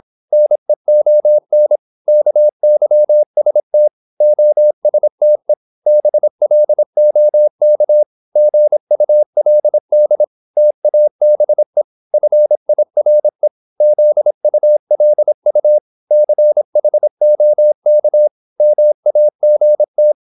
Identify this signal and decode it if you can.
Morse code, heard as a keyed tone, about 26 wpm, 600 Hz. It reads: NEON KYST OSTE BLOK GULD TABE FIRE ZULU CHOK MAGT
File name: Korte ord DK 26wpm